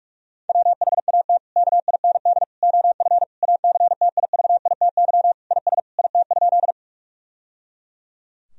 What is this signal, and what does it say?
Morse code, heard as a keyed tone, about 46 wpm, 700 Hz. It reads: WHAT KIND OF ACTIVITY IS IT?
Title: Sentences from Top 1000 Words - Review Part 4 - 45wpm